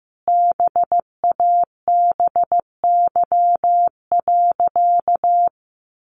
Morse code, heard as a keyed tone, about 15 wpm, 700 Hz. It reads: BABY.
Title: Sentences from Top 1500 Words 15wpm